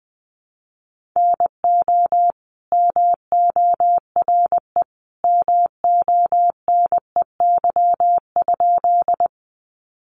Morse code, heard as a keyed tone, about 20 wpm, 700 Hz. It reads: NO MORE MONEY?
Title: Sentences from Top 400 Words 20wpm